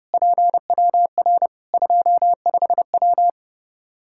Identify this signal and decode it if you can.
Morse code, heard as a keyed tone, about 30 wpm, 700 Hz. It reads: PWR 25W
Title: Practice QSO - Exchange 2 - Info - 30wpm